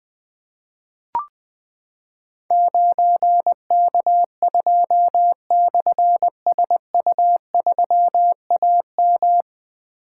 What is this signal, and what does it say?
Morse code, heard as a keyed tone, about 20 wpm, 700 Hz. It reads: 9K2/SU3AM
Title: Difficult Call Signs 20wpm